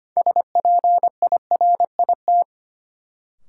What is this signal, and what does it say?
Morse code, heard as a keyed tone, about 25 wpm, 700 Hz. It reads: SPIRIT
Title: Sentences from Top 1400 Words - New Word Drill Alt - 25wpm